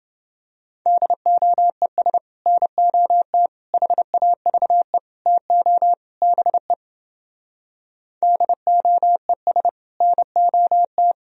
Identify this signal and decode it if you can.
Morse code, heard as a keyed tone, about 29 wpm, 700 Hz. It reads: DOES NOT HAVE TO BE DOES NOT